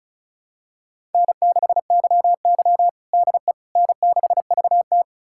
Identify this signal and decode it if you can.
Morse code, heard as a keyed tone, about 36 wpm, 700 Hz. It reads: N6YY DE N6VT
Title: USA Callsign DE Callsign - 4 Characters - 35wpm